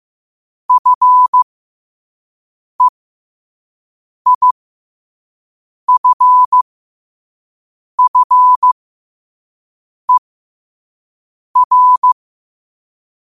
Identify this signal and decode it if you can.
Morse code, heard as a keyed tone, about 15 wpm, spaced as for 5 wpm, 1000 Hz. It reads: FEIFFER
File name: qotdfiles / 5wpm